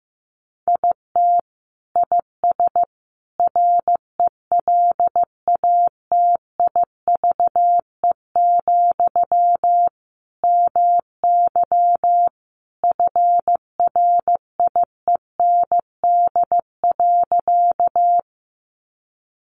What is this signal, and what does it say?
Morse code, heard as a keyed tone, about 15 wpm, 700 Hz. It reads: IT IS RELATIVE, MY FRIEND.
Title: Sentences from Top 1400 Words 15wpm